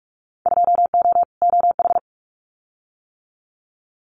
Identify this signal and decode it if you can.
Morse code, heard as a keyed tone, about 46 wpm, 700 Hz. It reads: 2O OH
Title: Winter Field Day Exchanges 45wpm